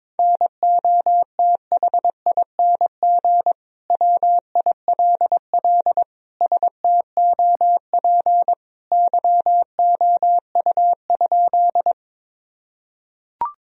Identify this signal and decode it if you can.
Morse code, heard as a keyed tone, about 21 wpm, 700 Hz. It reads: NOTHING WILL STOP YOU?
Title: Sentences from Top 1000 Words - Review Part 1 - 22wpm